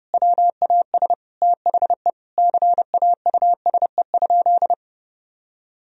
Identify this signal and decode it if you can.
Morse code, heard as a keyed tone, about 30 wpm, 700 Hz. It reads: WAS THE CAUSE?
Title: Sentences from Top 1000 Words - Review Part 1 - 30wpm